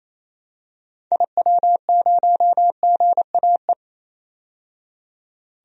Morse code, heard as a keyed tone, about 28 wpm, 700 Hz. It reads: IW0GAE
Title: DXCC-100 Call Signs - 6 Characters - 28wpm